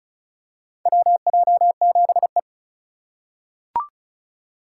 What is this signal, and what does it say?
Morse code, heard as a keyed tone, about 36 wpm, 700 Hz. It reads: WJ7E
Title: USA Call Signs - 4 Characters - 4x Character Spacing - 35wpm